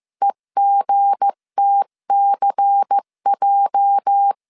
Morse code, heard as a keyed tone, about 15 wpm, 800 Hz. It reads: EGTCJ
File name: CW / DET / 1 9 1500C S B 14wpm 3302